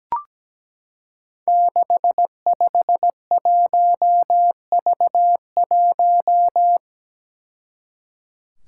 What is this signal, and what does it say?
Morse code, heard as a keyed tone, about 17 wpm, 700 Hz. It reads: 651V1
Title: Five Challenging Letter-Number Combinations 17wpm